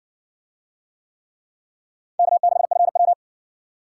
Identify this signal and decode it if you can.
Morse code, heard as a keyed tone, about 60 wpm, 700 Hz. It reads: K6FW